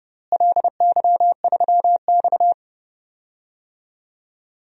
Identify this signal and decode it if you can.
Morse code, heard as a keyed tone, about 30 wpm, 700 Hz. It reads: LY3X
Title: DXCC-100 Call Signs - 4 Characters - Repeated 3x - 30wpm